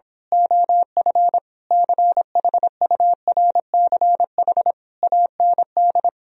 Morse code, heard as a keyed tone, about 26 wpm, 700 Hz. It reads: OF CHURCH AND
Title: Speed-Racing - Sets of 5 Words 22wpm